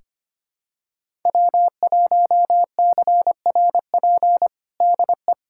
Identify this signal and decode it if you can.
Morse code, heard as a keyed tone, about 25 wpm, 700 Hz. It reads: W1CRP DE
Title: USA Callsign DE Callsign - 5 Characters - 25wpm